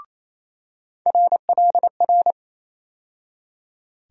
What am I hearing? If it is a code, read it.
Morse code, heard as a keyed tone, about 28 wpm, 700 Hz. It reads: RLR
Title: USA Call Sign Postfixes - 3 Letters 28wpm